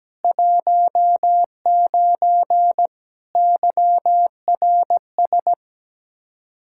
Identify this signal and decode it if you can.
Morse code, heard as a keyed tone, about 17 wpm, 700 Hz. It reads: 19 YRS
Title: Practice QSO - Exchange 3 - More Info - 17wpm